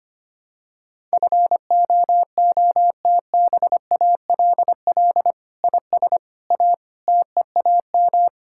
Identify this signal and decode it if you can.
Morse code, heard as a keyed tone, about 25 wpm, 700 Hz. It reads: FOOTBALL IS A TEAM
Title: Sentences from Top 1600 Words 25wpm